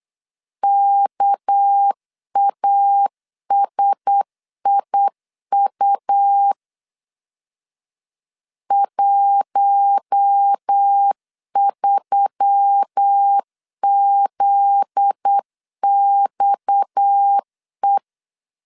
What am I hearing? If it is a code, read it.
Morse code, heard as a keyed tone, about 8 wpm, 800 Hz. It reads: KASIU 13ZXE
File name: LEZIONI / 7 GF9Z / 7 GF9Z 8wpm 3213